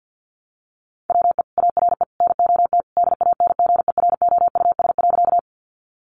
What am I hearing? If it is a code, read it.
Morse code, heard as a keyed tone, about 50 wpm, 700 Hz. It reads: WE ARE NOT DANGEROUS.
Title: Sentences from Top 1200 Words 50wpm